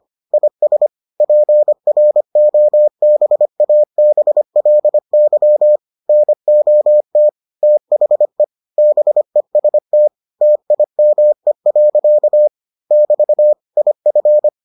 Morse code, heard as a keyed tone, about 25 wpm, 600 Hz. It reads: IS PROBABLY NOT THE BEST TIME. <BT> IF